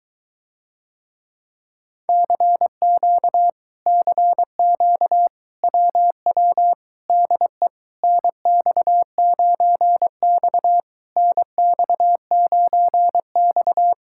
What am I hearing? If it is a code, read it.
Morse code, heard as a keyed tone, about 23 wpm, 700 Hz. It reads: CQ CQ WW DE NX9X NX9X